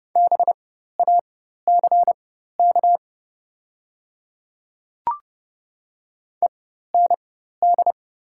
Morse code, heard as a keyed tone, about 30 wpm, spaced as for 14 wpm, 700 Hz. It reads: BACK END
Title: 4x Character Spacing - Top 200 Words - 30wpm